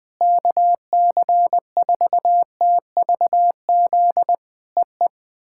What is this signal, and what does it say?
Morse code, heard as a keyed tone, about 20 wpm, 700 Hz. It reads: KC4TVZ EE
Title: POTA QSO Practice - 20wpm